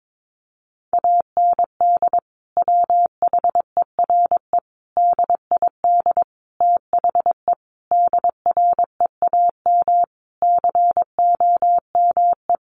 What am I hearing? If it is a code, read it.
Morse code, heard as a keyed tone, about 22 wpm, 700 Hz. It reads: AND WHERE DID THE DREAM COME
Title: Sentences from Top 1500 Words 22wpm